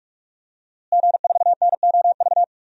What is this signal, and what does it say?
Morse code, heard as a keyed tone, about 42 wpm, 700 Hz. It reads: G4NOV
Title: DXCC-100 Call Signs - 5 Characters - Repeated 2x - 45wpm